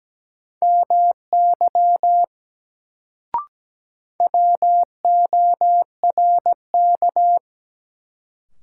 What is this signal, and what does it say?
Morse code, heard as a keyed tone, about 17 wpm, 700 Hz. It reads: MY WORK
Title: Sentences from Top 200 Words - New Word Drill - 17wpm